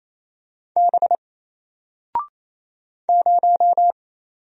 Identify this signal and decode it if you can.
Morse code, heard as a keyed tone, about 27 wpm, 700 Hz. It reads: B 0
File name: Single-Letter-Number-28wpm